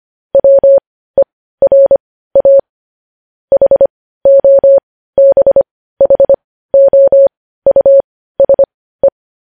Morse code, heard as a keyed tone, about 25 wpm, spaced as for 15 wpm, 550 Hz. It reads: WERA HOBHOUSE
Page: News Headlines in Morse Code at 15 WPM